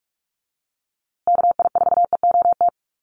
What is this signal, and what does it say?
Morse code, heard as a keyed tone, about 44 wpm, 700 Hz. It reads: KI4EOT
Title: USA Call Signs Repeated 3x 45wpm